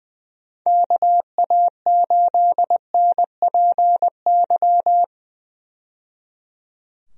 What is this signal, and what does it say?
Morse code, heard as a keyed tone, about 20 wpm, 700 Hz. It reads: KA8NPY
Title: USA Call Signs - 6 Characters - 20wpm